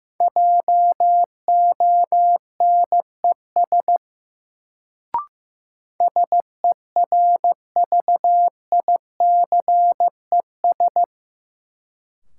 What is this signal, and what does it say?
Morse code, heard as a keyed tone, about 15 wpm, 700 Hz. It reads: JONES SERVICES
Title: Sets of 5 Words - New Word Drill - 15wpm